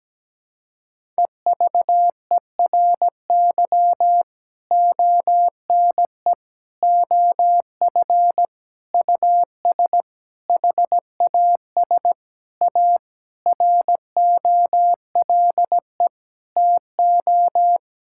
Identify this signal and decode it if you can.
Morse code, heard as a keyed tone, about 17 wpm, 700 Hz. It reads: EVERY ONE OF US HAS A ROLE TO
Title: Sentences from Top 1000 Words - Review Part 6 - 17wpm